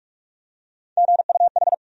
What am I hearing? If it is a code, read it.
Morse code, heard as a keyed tone, about 46 wpm, 700 Hz. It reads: GUH
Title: USA Call Sign Postfixes - 3 Letters 45wpm